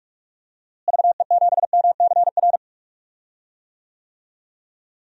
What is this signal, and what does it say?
Morse code, heard as a keyed tone, about 46 wpm, 700 Hz. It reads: VE7MCR
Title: DXCC-100 Call Signs - 6 Characters - Repeated 3x - 45wpm